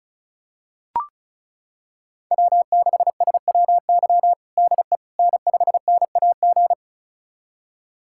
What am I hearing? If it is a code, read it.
Morse code, heard as a keyed tone, about 36 wpm, 700 Hz. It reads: W6SWY DE N5NAG
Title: USA Callsign DE Callsign 35wpm